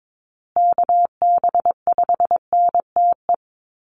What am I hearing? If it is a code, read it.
Morse code, heard as a keyed tone, about 22 wpm, 700 Hz. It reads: KB5NTE
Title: USA Call Signs Repeated 3x 22wpm